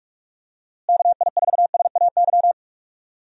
Morse code, heard as a keyed tone, about 42 wpm, 700 Hz. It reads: KI4SAY
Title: USA Call Signs - 6 Characters - Repeated 2x - 45wpm